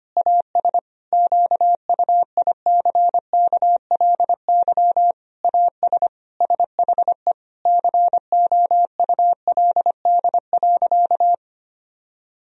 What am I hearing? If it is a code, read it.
Morse code, heard as a keyed tone, about 25 wpm, 700 Hz. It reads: AS QUICKLY AS SHE COULD.
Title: Sentences from Top 1000 Words - Review Part 3 - 25wpm